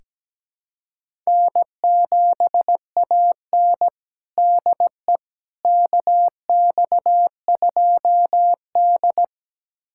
Morse code, heard as a keyed tone, about 17 wpm, 700 Hz. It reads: N7AN DE KX2D
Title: USA Callsign DE Callsign - 4 Characters - 17wpm